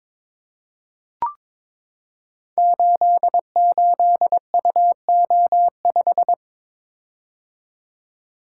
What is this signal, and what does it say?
Morse code, heard as a keyed tone, about 22 wpm, 690 Hz. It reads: 88UO5
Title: Code Groups - 5 Letter:Numbers - 22wpm